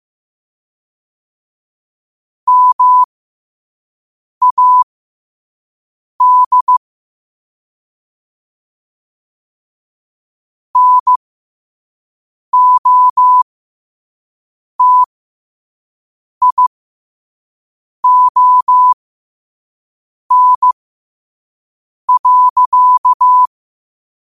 Morse code, heard as a keyed tone, about 15 wpm, spaced as for 5 wpm, 1000 Hz. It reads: MAD NOTION.
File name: qotdfiles / 5wpm